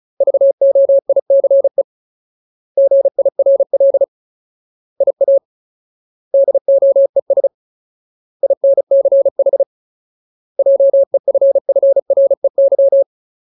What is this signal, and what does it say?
Morse code, heard as a keyed tone, about 34 wpm, 550 Hz. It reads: VOICE GIRL IA DOES INCH JEFFREY
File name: Ultimate Word List / 35WPM Ultimate